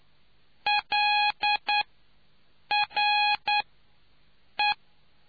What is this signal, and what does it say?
Morse code, heard as a keyed tone, about 9 wpm, 830 Hz. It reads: L R E